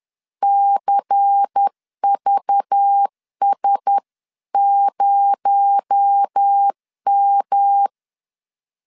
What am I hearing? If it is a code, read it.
Morse code, heard as a keyed tone, about 10 wpm, 800 Hz. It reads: CVS0M
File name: LEZIONI / 9 28 10wpm 2842